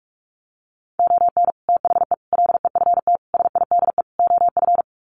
Morse code, heard as a keyed tone, about 46 wpm, 700 Hz. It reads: ON THE LEFT SIDE OF
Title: Sets of 5 Words Encore2 45wpm